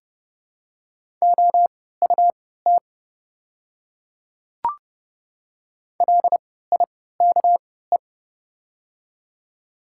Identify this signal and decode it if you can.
Morse code, heard as a keyed tone, about 32 wpm, spaced as for 17 wpm, 700 Hz. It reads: OUT LIKE
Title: 3x Character Spacing - Top 100 Words - 30wpm